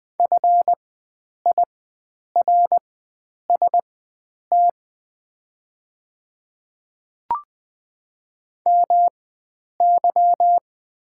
Morse code, heard as a keyed tone, about 20 wpm, spaced as for 9 wpm, 700 Hz. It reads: FIRST MY
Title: 4x Character Spacing - Top 300 Words - 20wpm